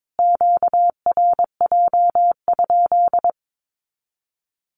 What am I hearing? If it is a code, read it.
Morse code, heard as a keyed tone, about 22 wpm, 700 Hz. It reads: QRJ?
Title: Q-Codes Repeated 3x 22wpm